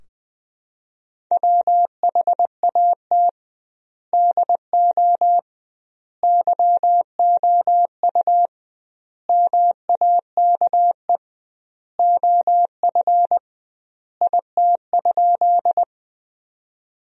Morse code, heard as a keyed tone, about 20 wpm, 700 Hz. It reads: WHAT DO YOU MAKE OF IT?
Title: 2x Word-Spacing - Sentences from Top 100 Words - 20wpm